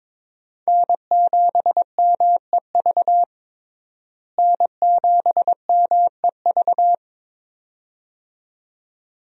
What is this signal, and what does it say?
Morse code, heard as a keyed tone, about 22 wpm, 700 Hz. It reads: N7MEV N7MEV
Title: USA Call Signs - 5 Characters - Repeated 2x - 22wpm